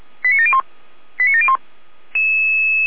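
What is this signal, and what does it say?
Five-tone selective call (ZVEI), heard as quick runs of five tones, tones of about 70 ms. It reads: 89701 89701
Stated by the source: Received on an SDR.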